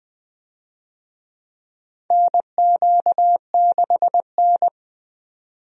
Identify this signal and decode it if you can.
Morse code, heard as a keyed tone, about 21 wpm, 700 Hz. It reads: NQ6N
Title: USA Call Signs - 4 Characters - 20wpm